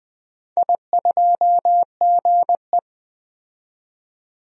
Morse code, heard as a keyed tone, about 20 wpm, 700 Hz. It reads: I2GE
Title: DXCC-100 Call Signs - 4 Characters - 20wpm